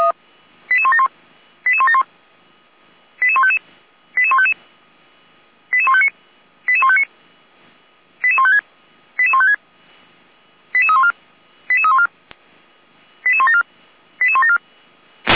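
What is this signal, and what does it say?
Five-tone selective call (ZVEI), heard as quick runs of five tones, tones of about 70 ms. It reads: 88171 88171 88155 88155 88159 88159 88156 88156 88314 88314 88174 88174
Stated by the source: ZVEI five-tone selective call 88090 received by SDR.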